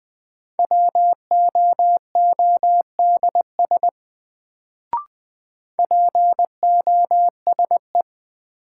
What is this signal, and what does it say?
Morse code, heard as a keyed tone, about 20 wpm, 700 Hz. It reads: WOODS POSE
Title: Sentences from Top 1000 Words - Review Part 2 - New Word Drill - 20wpm